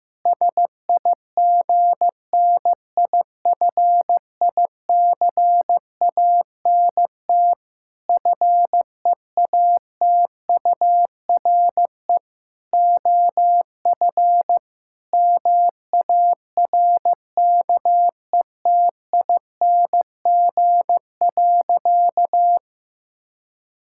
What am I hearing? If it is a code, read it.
Morse code, heard as a keyed tone, about 15 wpm, 700 Hz. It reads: SIGNIFICANT FEATURE OF MARKETING.
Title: Sentences from Top 1000 Words - Review Part 6 - 15wpm